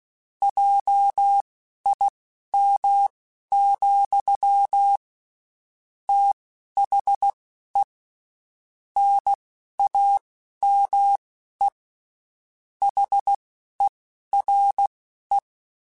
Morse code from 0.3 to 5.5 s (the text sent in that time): JIM,